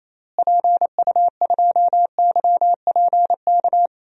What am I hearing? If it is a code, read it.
Morse code, heard as a keyed tone, about 27 wpm, 700 Hz. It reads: PU2YPK
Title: DXCC-100 Call Signs - 6 Characters - Repeated 2x - 28wpm